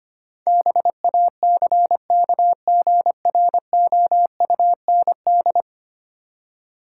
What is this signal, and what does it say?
Morse code, heard as a keyed tone, about 25 wpm, 700 Hz. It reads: BACKGROUND
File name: 10 Letter Words - First 3 Letters Given - 25wpm